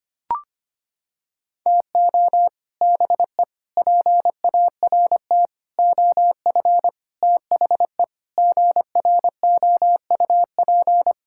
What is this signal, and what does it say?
Morse code, heard as a keyed tone, about 25 wpm, 700 Hz. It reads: TO BE PART OF THE GROUP
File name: practice / 25wpm / Sentences with 4-Word Phrases - 25wpm